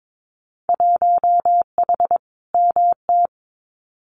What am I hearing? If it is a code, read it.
Morse code, heard as a keyed tone, about 22 wpm, 700 Hz. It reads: 1H MT
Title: Winter Field Day Exchanges Encore 22wpm